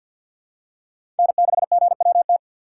Morse code, heard as a keyed tone, about 49 wpm, 700 Hz. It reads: N6GWT
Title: USA Call Signs - 5 Characters - 6x Character Spacing - 50wpm